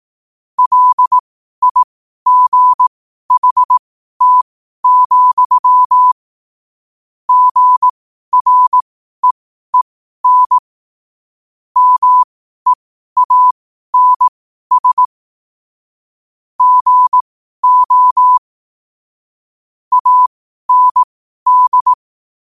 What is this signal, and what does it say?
Morse code, heard as a keyed tone, about 18 wpm, spaced as for 12 wpm, 1000 Hz. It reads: LIGHT, GREEN MEANS GO AND
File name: qotdfiles / 12wpm